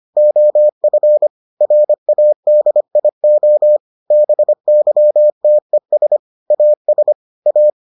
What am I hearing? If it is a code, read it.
Morse code, heard as a keyed tone, about 25 wpm, 600 Hz. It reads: OF RADIO BYTES AS A